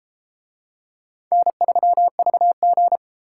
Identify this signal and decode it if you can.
Morse code, heard as a keyed tone, about 31 wpm, 700 Hz. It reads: N3VG